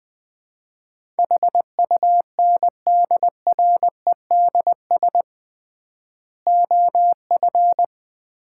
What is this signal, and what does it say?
Morse code, heard as a keyed tone, about 20 wpm, 700 Hz. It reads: HUNDREDS OF
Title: 3x Word-Spacing - Sets of 4 Words - 20wpm